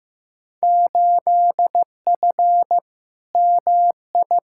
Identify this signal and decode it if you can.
Morse code, heard as a keyed tone, about 15 wpm, 700 Hz. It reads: 8F MI
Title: Field Day Exchanges Encore - 15wpm